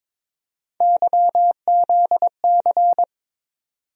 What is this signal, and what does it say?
Morse code, heard as a keyed tone, about 22 wpm, 700 Hz. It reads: YZC
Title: USA Call Sign Postfixes - 3 Letters 22wpm